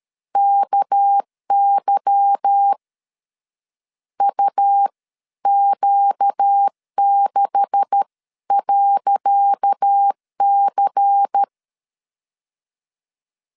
Morse code, heard as a keyed tone, about 13 wpm, spaced as for 11 wpm, 800 Hz. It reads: KY UQ6.C
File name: LEZIONI / 10 SIGNS / 10C DOT 12wpm 2451